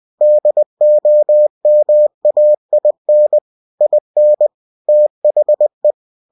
Morse code, heard as a keyed tone, about 20 wpm, 600 Hz. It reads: DOMAIN IN THE
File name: cw-books / winnie-the-pooh / winnie-20wpm-page-0114